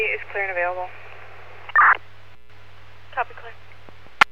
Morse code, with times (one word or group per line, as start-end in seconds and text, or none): none